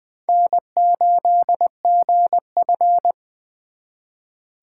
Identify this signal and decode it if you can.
Morse code, heard as a keyed tone, about 24 wpm, 700 Hz. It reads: N8GF
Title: Speed Racing - USA 1x2 Call Signs 20wpm